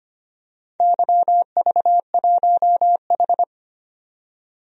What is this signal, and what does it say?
Morse code, heard as a keyed tone, about 25 wpm, 700 Hz. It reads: YV1H
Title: DXCC-100 Call Signs - 4 Characters - 25wpm